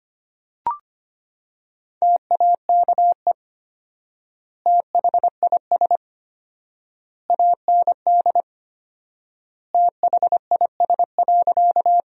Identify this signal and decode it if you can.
Morse code, heard as a keyed tone, about 25 wpm, 700 Hz. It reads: TAKE THIS AND THIS.
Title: 4x Word-Spacing - Sentences from Top 200 Words - 25wpm